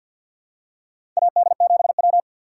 Morse code, heard as a keyed tone, about 52 wpm, 700 Hz. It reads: AD7W
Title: USA Call Signs - 2x Character Spacing - 50wpm